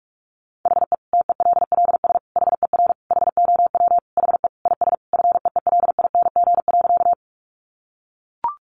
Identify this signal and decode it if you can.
Morse code, heard as a keyed tone, about 44 wpm, 700 Hz. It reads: HE TELLS HER HOW HE IS FEELING.
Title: Additional Easy Phrases 45wpm